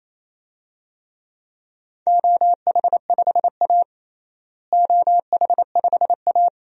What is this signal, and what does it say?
Morse code, heard as a keyed tone, about 27 wpm, 700 Hz. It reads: OH5A OH5A
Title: DXCC-100 Call Signs - 4 Characters - Repeated 3x - 28wpm